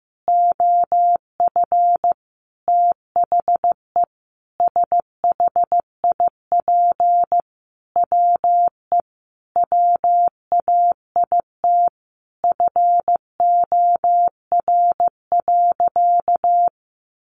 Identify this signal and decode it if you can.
Morse code, heard as a keyed tone, about 15 wpm, 700 Hz. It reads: OF THE SHIP WE WAIT FOR.
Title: Sentences from Top 800 Words 15wpm